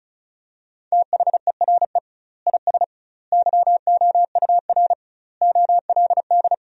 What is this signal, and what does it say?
Morse code, heard as a keyed tone, about 36 wpm, 700 Hz. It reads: THERE IS YOUR OLD
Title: 2x Word-Spacing - Sentences from Top 500 Words - 35wpm